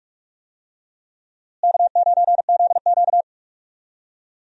Morse code, heard as a keyed tone, about 46 wpm, 700 Hz. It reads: K9ZQ
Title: USA Call Signs - 4 Characters - Repeated 2x - 45wpm